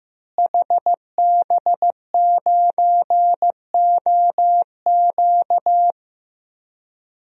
Morse code, heard as a keyed tone, about 15 wpm, 700 Hz. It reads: HB9OQ
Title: DXCC-100 Call Signs - 5 Characters - Repeated 3x - 15wpm